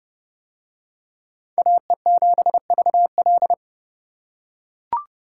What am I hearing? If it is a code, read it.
Morse code, heard as a keyed tone, about 30 wpm, 700 Hz. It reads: AE7VL